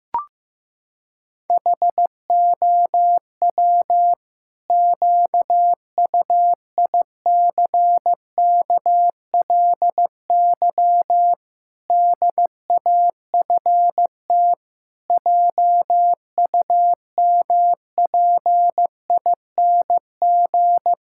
Morse code, heard as a keyed tone, about 15 wpm, 700 Hz. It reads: HOW QUICKLY DAFT JUMPING
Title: Pangram Challenge v2 - 15wpm